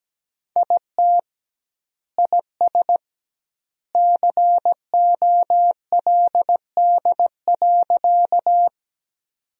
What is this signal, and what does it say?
Morse code, heard as a keyed tone, about 17 wpm, 700 Hz. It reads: IT IS COLD.
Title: 2x Word-Spacing - Sentences from Top 400 Words - 17wpm